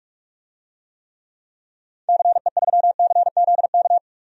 Morse code, heard as a keyed tone, about 45 wpm, 700 Hz. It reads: CE3CZK
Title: DXCC-100 Call Signs - 6 Characters - 45wpm